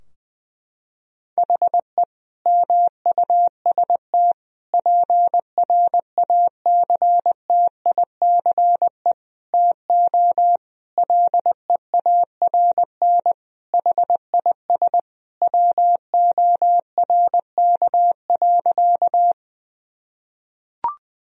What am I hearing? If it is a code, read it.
Morse code, heard as a keyed tone, about 21 wpm, 700 Hz. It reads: HE MUST PRACTICE TO LEARN HIS WORK.
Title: Sentences from Top 1000 Words - Review Part 4 - 20wpm